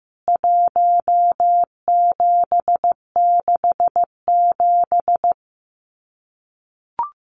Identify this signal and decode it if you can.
Morse code, heard as a keyed tone, about 15 wpm, 700 Hz. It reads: 1767
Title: CWT Exchange 15wpm